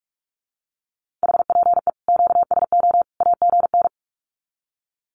Morse code, heard as a keyed tone, about 44 wpm, 700 Hz. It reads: HPE QSO AGN
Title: Common QSO Phrases 45wpm